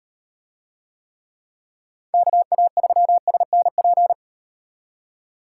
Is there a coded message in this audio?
Yes: Morse code from 2.1 to 4.1 s, heard as a keyed tone, about 35 wpm, 700 Hz. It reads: KA3SNP